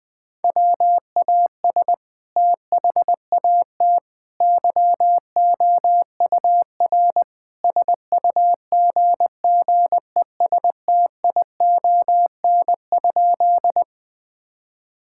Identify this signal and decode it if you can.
Morse code, heard as a keyed tone, about 20 wpm, 700 Hz. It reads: WAS THAT YOUR SUGGESTION?
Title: Sentences from Top 1600 Words 20wpm